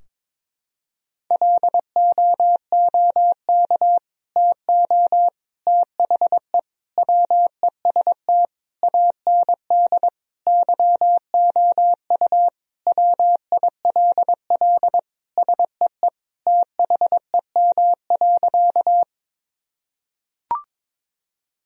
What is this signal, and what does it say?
Morse code, heard as a keyed tone, about 22 wpm, 700 Hz. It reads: LOOK TO THE WEST AND YOU WILL SEE THEM.
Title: Sentences from Top 1000 Words - Review Part 2 - 22wpm